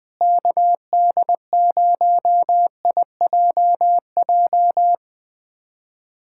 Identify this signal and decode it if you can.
Morse code, heard as a keyed tone, about 20 wpm, 700 Hz. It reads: KD0IJJ
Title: USA Callsign DE Callsign - 6 Characters - 20wpm